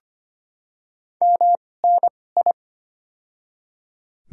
Morse code, heard as a keyed tone, about 25 wpm, spaced as for 18 wpm, 700 Hz. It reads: MNI
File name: Top 90 QSO Elements - 2x Character Spacing - 25wpm